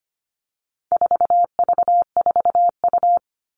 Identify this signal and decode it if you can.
Morse code, heard as a keyed tone, about 25 wpm, 700 Hz. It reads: 4V4U
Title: All Cut Numbers - 4-Digit 25wpm